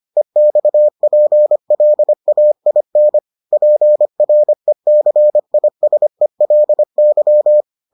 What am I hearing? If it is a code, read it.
Morse code, heard as a keyed tone, about 25 wpm, 600 Hz. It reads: EXPLAIN PRECISELY